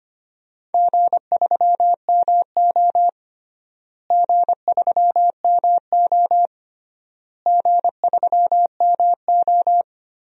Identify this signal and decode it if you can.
Morse code, heard as a keyed tone, about 25 wpm, 700 Hz. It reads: G3MO G3MO G3MO
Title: DXCC-100 Call Signs - 4 Characters - Repeated 3x - 25wpm